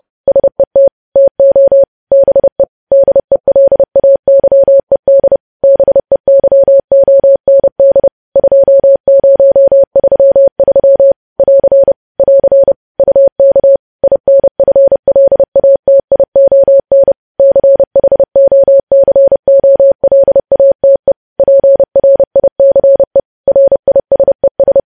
Morse code, heard as a keyed tone, about 30 wpm, 550 Hz. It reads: SET TO BE DELAYED BEYOND 2033 <AR> <AR> UK INFLATION CHOCOLATE PRICE RISES
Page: news headlines in morse code at 30 wpm